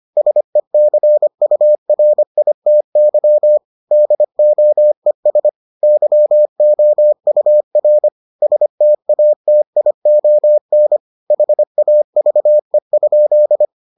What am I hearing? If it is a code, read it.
Morse code, heard as a keyed tone, about 25 wpm, 600 Hz. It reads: SECURITY DOES YOUR STATION HAVE?